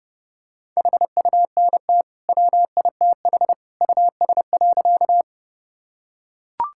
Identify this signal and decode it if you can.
Morse code, heard as a keyed tone, about 32 wpm, 700 Hz. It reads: HUNT WITH US.
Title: Sentences from Top 1600 Words 30wpm